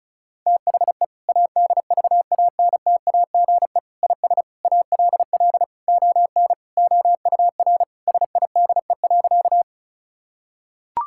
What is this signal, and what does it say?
Morse code, heard as a keyed tone, about 36 wpm, 700 Hz. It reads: THE ADVANTAGE IS ALL ON OUR SIDE.
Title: Sentences from Top 1000 Words - Review Part 6 - 35wpm